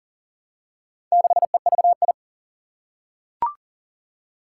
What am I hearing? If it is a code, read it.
Morse code, heard as a keyed tone, about 40 wpm, 700 Hz. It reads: 6EVI
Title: Four Challenging Letter-Number Combinations 40wpm